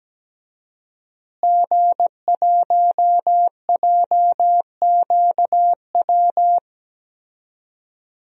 Morse code, heard as a keyed tone, about 17 wpm, 700 Hz. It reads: G1JQW
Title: DXCC-100 Call Signs - 5 Characters - Repeated 3x - 17wpm